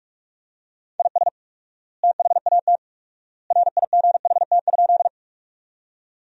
Morse code, heard as a keyed tone, about 46 wpm, 700 Hz. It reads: IS THAT RIGHT?
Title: 4x Word-Spacing - Sentences from Top 200 Words - 45wpm